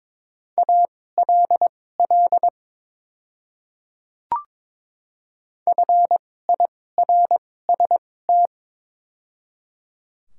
Morse code, heard as a keyed tone, about 22 wpm, spaced as for 16 wpm, 690 Hz. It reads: ALL FIRST
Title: 2x Character Spacing - Top 100 Words - 22wpm